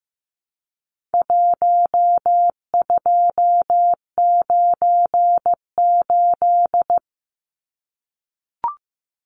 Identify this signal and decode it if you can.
Morse code, heard as a keyed tone, about 15 wpm, 700 Hz. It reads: 1298
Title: Numbers - 4-Digit 15wpm